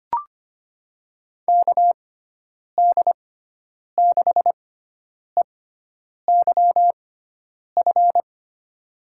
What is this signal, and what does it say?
Morse code, heard as a keyed tone, about 25 wpm, spaced as for 9 wpm, 700 Hz. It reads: KD6EYF
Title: USA Call Signs - 6 Characters - 6x Character Spacing - 25wpm